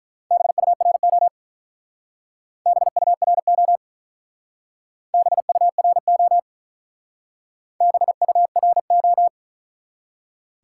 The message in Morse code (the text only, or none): BURO BURO BURO BURO